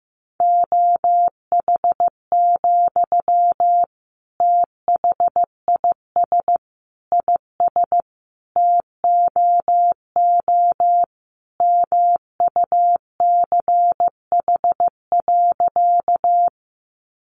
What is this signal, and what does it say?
Morse code, heard as a keyed tone, about 15 wpm, 700 Hz. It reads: OH, THIS IS TOO MUCH.
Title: Sentences from Top 500 Words 15wpm